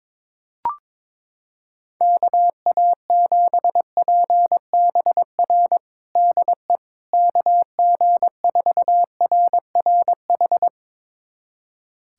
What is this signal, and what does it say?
Morse code, heard as a keyed tone, about 22 wpm, 700 Hz. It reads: KA7PBR DE KG4RRH
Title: USA Callsign DE Callsign - 6 Characters - 22wpm